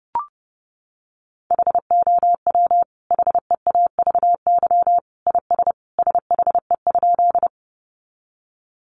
Morse code, heard as a keyed tone, about 30 wpm, 700 Hz. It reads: HOW HEAVY IS SHE?
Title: Sentences from Top 1100 Words 30wpm